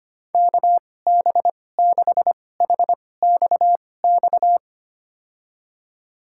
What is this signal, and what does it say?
Morse code, heard as a keyed tone, about 25 wpm, spaced as for 18 wpm, 700 Hz. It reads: KB6HXX
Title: USA Call Signs - 6 Characters - 2x Character Spacing - 25wpm